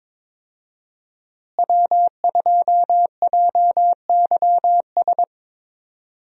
Morse code, heard as a keyed tone, about 22 wpm, 700 Hz. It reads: W2JYS
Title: USA Call Signs - 5 Characters - Repeated 2x - 22wpm